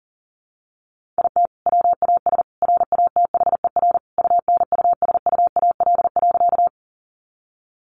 Morse code, heard as a keyed tone, about 40 wpm, 700 Hz. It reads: IT WAS RATHER UNUSUAL.
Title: Sentences from Top 1400 Words 40wpm